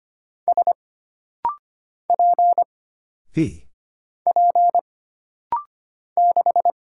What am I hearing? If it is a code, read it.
Morse code, heard as a keyed tone, about 25 wpm, 700 Hz. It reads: S P P 6
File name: Single Character (Letter:Number:Punctuation:Prosign) - Rapid-Fire - 25wpm